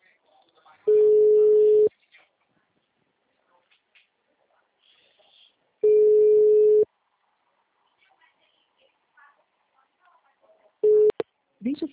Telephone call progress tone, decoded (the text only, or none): ringback tone